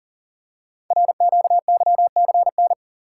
Morse code, heard as a keyed tone, about 38 wpm, 690 Hz. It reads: RQYCN
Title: Code Groups - 5 Letters - 40wpm